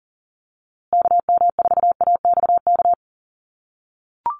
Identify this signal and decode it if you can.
Morse code, heard as a keyed tone, about 40 wpm, 700 Hz. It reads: KM4AXK
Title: USA Call Signs Repeated 3x 40wpm